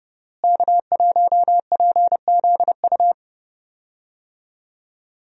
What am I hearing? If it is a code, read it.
Morse code, heard as a keyed tone, about 30 wpm, 700 Hz. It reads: K1PZU
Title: USA Call Signs - 5 Characters - Repeated 2x - 30wpm